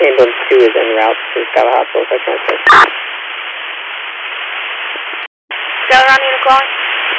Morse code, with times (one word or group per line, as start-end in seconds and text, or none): none